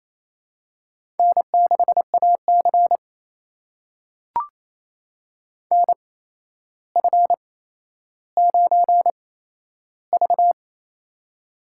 Morse code, heard as a keyed tone, about 27 wpm, spaced as for 12 wpm, 700 Hz. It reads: N6AC NF9V
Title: USA Call Signs - 4 Characters - 8x Character Spacing - 28wpm